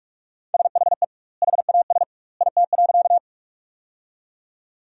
Morse code, heard as a keyed tone, about 45 wpm, 700 Hz. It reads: SHE HAS IT.
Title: 2x Word-Spacing - Sentences from Top 100 Words - 45wpm